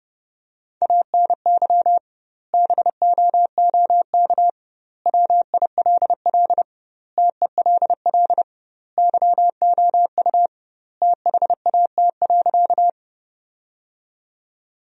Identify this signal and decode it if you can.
Morse code, heard as a keyed tone, about 30 wpm, 700 Hz. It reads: ANY BOOK WILL TELL YOU THAT.
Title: 2x Word-Spacing - Sentences from Top 300 Words - 30wpm